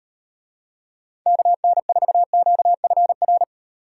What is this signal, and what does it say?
Morse code, heard as a keyed tone, about 38 wpm, 700 Hz. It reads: KN4QFR